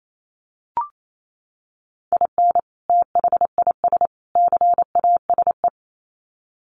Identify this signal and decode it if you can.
Morse code, heard as a keyed tone, about 28 wpm, 700 Hz. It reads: IN THIS CASE
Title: Sets of 3 Words 28wpm